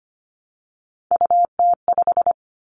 Morse code, heard as a keyed tone, about 25 wpm, 700 Hz. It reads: UT5
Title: Callsign Prefixes for Top 25 DXCC - 25wpm